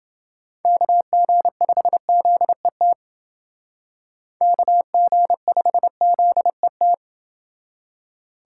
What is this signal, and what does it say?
Morse code, heard as a keyed tone, about 29 wpm, 700 Hz. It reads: KG5ZET KG5ZET